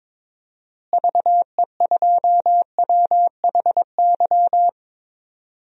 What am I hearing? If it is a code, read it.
Morse code, heard as a keyed tone, about 21 wpm, 700 Hz. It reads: VE2WHY
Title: DXCC-100 Call Signs - 6 Characters - Repeated 3x - 22wpm